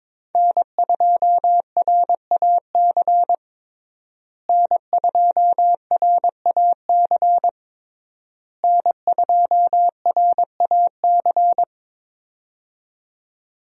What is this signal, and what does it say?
Morse code, heard as a keyed tone, about 21 wpm, 700 Hz. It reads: N2RAC N2RAC N2RAC
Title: USA Call Signs - 5 Characters - Repeated 3x - 22wpm